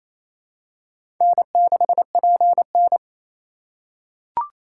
Morse code, heard as a keyed tone, about 27 wpm, 700 Hz. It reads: N6PN
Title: USA Call Signs - 4 Characters - 4x Character Spacing - 28wpm